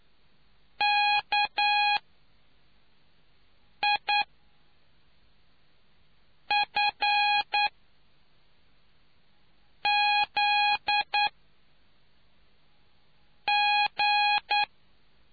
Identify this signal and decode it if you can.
Morse code, heard as a keyed tone, about 9 wpm, spaced as for 3 wpm, 830 Hz. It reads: KIFZG